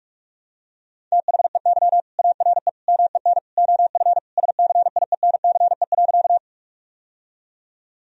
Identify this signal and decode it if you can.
Morse code, heard as a keyed tone, about 41 wpm, 700 Hz. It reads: THEY ARE MEN OF SCIENCE.
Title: Sentences from Top 1000 Words - Review Part 2 - 45wpm